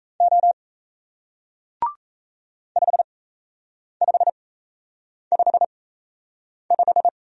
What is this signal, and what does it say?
Morse code, heard as a keyed tone, about 37 wpm, spaced as for 8 wpm, 700 Hz. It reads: O 5555